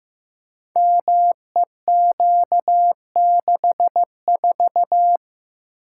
Morse code, heard as a keyed tone, about 15 wpm, 700 Hz. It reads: MEQ64
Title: Code Groups - 5 Letter:Numbers - 15wpm